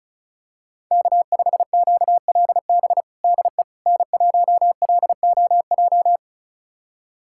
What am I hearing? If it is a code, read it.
Morse code, heard as a keyed tone, about 36 wpm, 700 Hz. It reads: K5QLB DE N1LOJ
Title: USA Callsign DE Callsign - 5 Characters - 35wpm